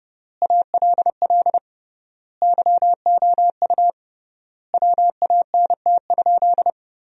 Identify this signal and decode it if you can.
Morse code, heard as a keyed tone, about 29 wpm, 700 Hz. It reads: ALL YOU WANT?
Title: 3x Word-Spacing - Sentences from Top 200 Words - 30wpm